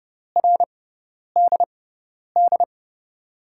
Morse code, heard as a keyed tone, about 32 wpm, spaced as for 10 wpm, 700 Hz. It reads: RDD
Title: USA Call Signs - 5 Characters - 6x Character Spacing - 30wpm